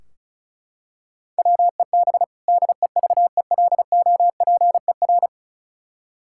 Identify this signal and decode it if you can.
Morse code, heard as a keyed tone, about 36 wpm, 700 Hz. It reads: WEB DEVELOPER
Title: Practice QSO - Exchange 3 - More Info - 35wpm